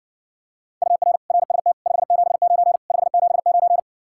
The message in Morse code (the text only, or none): UR RST 579 579